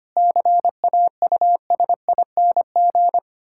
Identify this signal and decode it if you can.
Morse code, heard as a keyed tone, about 25 wpm, 700 Hz. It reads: CAUSING
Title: Pangram Challenge v2 - 25wpm